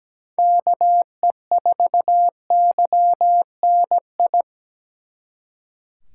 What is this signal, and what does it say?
Morse code, heard as a keyed tone, about 20 wpm, 700 Hz. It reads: KE4YNI
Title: Speed Racing - USA 2x3 Call Signs 17wpm